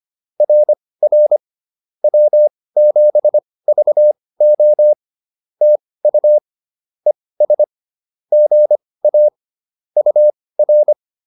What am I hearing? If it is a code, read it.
Morse code, heard as a keyed tone, about 25 wpm, spaced as for 18 wpm, 600 Hz. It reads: RR W7VO TU ES GA UR